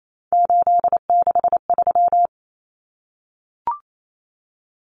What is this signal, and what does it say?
Morse code, heard as a keyed tone, about 28 wpm, 700 Hz. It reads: 863
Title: Common Cut Numbers - 3-Digit 28wpm